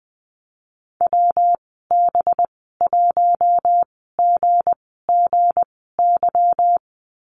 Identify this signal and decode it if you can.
Morse code, heard as a keyed tone, about 19 wpm, spaced as for 14 wpm, 700 Hz. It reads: WB1GGY
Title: 219 Lesson 35g - Number 0 - Call Signs 20wpm